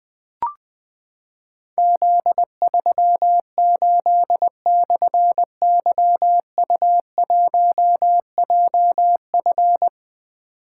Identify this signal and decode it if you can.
Morse code, heard as a keyed tone, about 20 wpm, 700 Hz. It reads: Z38/YU1JF
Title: Difficult Call Signs 20wpm